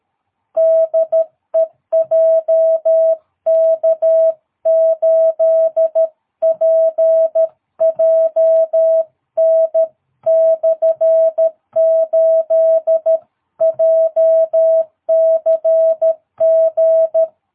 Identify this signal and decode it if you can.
Morse code, heard as a keyed tone, about 13 wpm, 660 Hz. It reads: DE JK8PJN/8 JCG
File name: jh8cbh / 2022cw